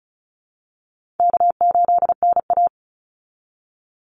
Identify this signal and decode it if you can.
Morse code, heard as a keyed tone, about 34 wpm, 700 Hz. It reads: K8NA
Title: Easy Call Signs Repeated 2x - 35wpm